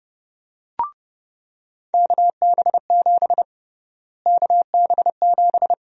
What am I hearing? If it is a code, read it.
Morse code, heard as a keyed tone, about 29 wpm, 700 Hz. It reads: KB7 KB7
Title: USA Call Sign Prefixes - Repeated 2x - 30wpm